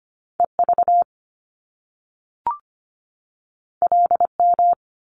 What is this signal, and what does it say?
Morse code, heard as a keyed tone, about 25 wpm, 700 Hz. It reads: EV LM
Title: Top Two Letter Combinations in Words 25wpm